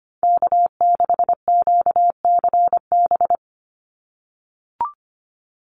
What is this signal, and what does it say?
Morse code, heard as a keyed tone, about 25 wpm, 700 Hz. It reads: K6QCB
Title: NA SOTA Chaser Callsigns 25wpm